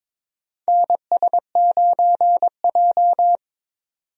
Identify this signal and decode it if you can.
Morse code, heard as a keyed tone, about 23 wpm, 700 Hz. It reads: NS9J
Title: USA Call Signs - 4 Characters - 22wpm